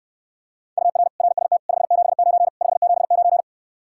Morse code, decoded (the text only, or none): UR RST 579 579